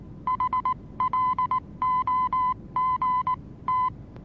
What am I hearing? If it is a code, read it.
Morse code, heard as a keyed tone, about 19 wpm, spaced as for 16 wpm, 1010 Hz. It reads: HLOGT
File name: repos / Morsekurs / DJ4UF Morsen26